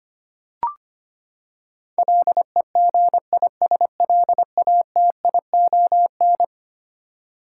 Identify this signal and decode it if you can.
Morse code, heard as a keyed tone, about 25 wpm, 700 Hz. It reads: LEGISLATION
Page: morse-code-ninja-practice-25wpm